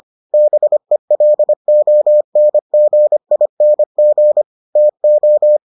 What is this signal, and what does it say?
Morse code, heard as a keyed tone, about 25 wpm, 600 Hz. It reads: BELONGING TO